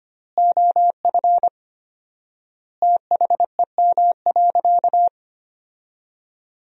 Morse code, heard as a keyed tone, about 25 wpm, 700 Hz. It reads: OF THEM.
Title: 4x Word-Spacing - Sentences from Top 300 Words - 25wpm